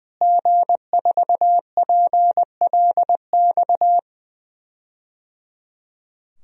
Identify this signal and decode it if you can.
Morse code, heard as a keyed tone, about 20 wpm, 700 Hz. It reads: G4PLX
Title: DXCC-100 Call Signs - 5 Characters - Repeated 3x - 20wpm